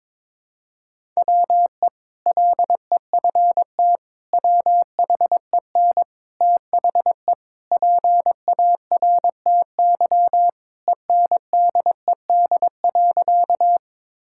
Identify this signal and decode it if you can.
Morse code, heard as a keyed tone, about 22 wpm, 700 Hz. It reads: WE LEFT WHEN THE PARTY ENDED.
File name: practice / 22wpm / Sentences with 2-Word Phrases - 22wpm